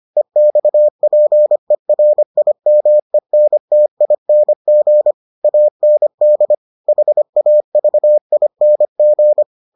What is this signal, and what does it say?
Morse code, heard as a keyed tone, about 25 wpm, 600 Hz. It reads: EXPERIMENTING AND HAVING